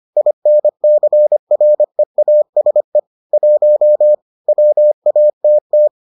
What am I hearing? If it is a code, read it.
Morse code, heard as a keyed tone, about 25 wpm, 600 Hz. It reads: INCREASE 1 WATT